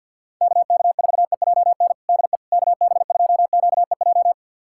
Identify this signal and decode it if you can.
Morse code, heard as a keyed tone, about 50 wpm, 700 Hz. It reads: KK4EJN DE KB2QEJ
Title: USA Callsign DE Callsign 50wpm